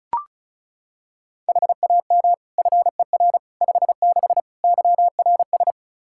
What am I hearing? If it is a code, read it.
Morse code, heard as a keyed tone, about 36 wpm, 700 Hz. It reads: HAM FER 56 YRS
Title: K1USN SST QSO Practice 35wpm